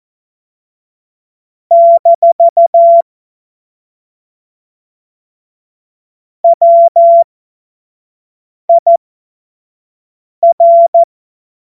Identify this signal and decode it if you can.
Morse code, heard as a keyed tone, about 14 wpm, spaced as for 5 wpm, 680 Hz. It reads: - WIR
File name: Witz-21-Oktober-2025-25-F70